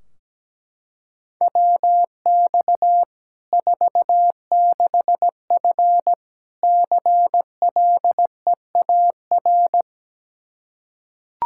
Morse code, heard as a keyed tone, about 17 wpm, 700 Hz. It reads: WX 46F CLEAR
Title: Practice QSO - Exchange 2 - Info - 17wpm